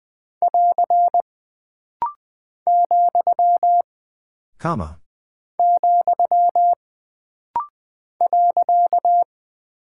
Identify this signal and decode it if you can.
Morse code, heard as a keyed tone, about 20 wpm, 700 Hz. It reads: <AR> , , .
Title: Single Punctuation-Prosign - Rapid-Fire - 20wpm